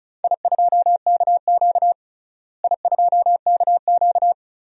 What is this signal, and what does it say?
Morse code, heard as a keyed tone, about 34 wpm, 700 Hz. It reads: I2KQ I2KQ
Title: DXCC-100 Call Signs - 4 Characters - Repeated 3x - 35wpm